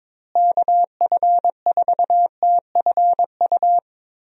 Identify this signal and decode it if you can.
Morse code, heard as a keyed tone, about 21 wpm, 700 Hz. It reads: KF4TFU
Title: USA Call Signs - 6 Characters - Repeated 3x - 22wpm